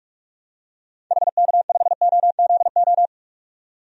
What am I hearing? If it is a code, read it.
Morse code, heard as a keyed tone, about 45 wpm, 700 Hz. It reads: HK5OZO
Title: DXCC-100 Call Signs - 6 Characters - 45wpm